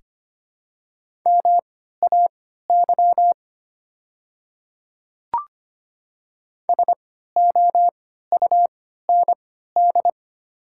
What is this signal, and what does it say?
Morse code, heard as a keyed tone, about 25 wpm, spaced as for 14 wpm, 690 Hz. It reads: MAY SOUND
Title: 3x Character Spacing - Top 100 Words - 25wpm